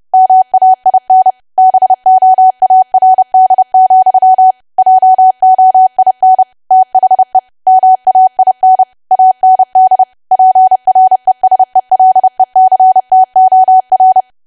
Morse code, heard as a keyed tone, about 30 wpm, 750 Hz. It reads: MAIN BOARD, JOIN THE MAIN AND PRESELECTOR
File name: cwa / W1AW-Level3 / 30WPM